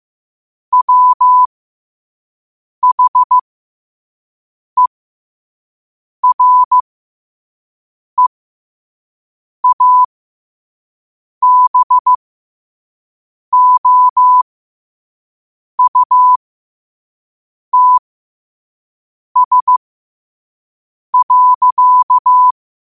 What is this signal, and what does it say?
Morse code, heard as a keyed tone, about 15 wpm, spaced as for 5 wpm, 1000 Hz. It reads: WHEREABOUTS.
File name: qotdfiles / 5wpm